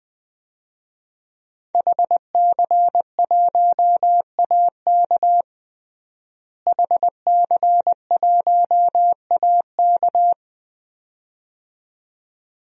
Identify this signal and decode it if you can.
Morse code, heard as a keyed tone, about 20 wpm, 700 Hz. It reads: HC1AK HC1AK
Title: DXCC-100 Call Signs - 5 Characters - Repeated 2x - 20wpm